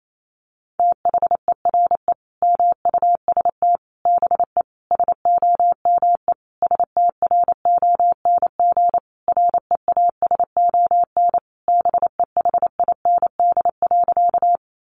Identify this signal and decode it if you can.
Morse code, heard as a keyed tone, about 28 wpm, 700 Hz. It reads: THERE MUST BE SOME STRONG REASON BEHIND.
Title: Sentences from Top 700 Words 28wpm